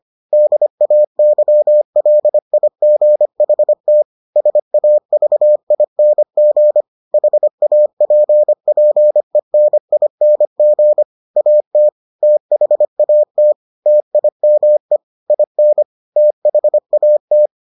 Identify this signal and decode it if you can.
Morse code, heard as a keyed tone, about 25 wpm, 600 Hz. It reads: DAYLIGHT SAVING HAPPENING AT THAT TIME IN THAT